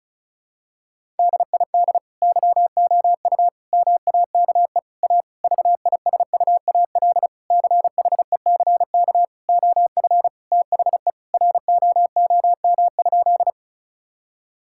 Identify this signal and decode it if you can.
Morse code, heard as a keyed tone, about 36 wpm, 700 Hz. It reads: DID YOU MAKE A VISUAL CHECK OF THE ROOM?
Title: Sentences from Top 1000 Words - Review Part 6 - 35wpm